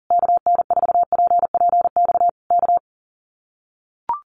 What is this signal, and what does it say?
Morse code, heard as a keyed tone, about 40 wpm, 700 Hz. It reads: KN4PPX K
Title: Common QSO Phrases Encore 40wpm